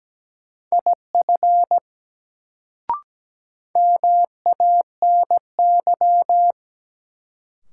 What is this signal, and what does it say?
Morse code, heard as a keyed tone, about 17 wpm, 700 Hz. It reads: IF MANY
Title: Sets of 2 Words - New Word Drill - 17wpm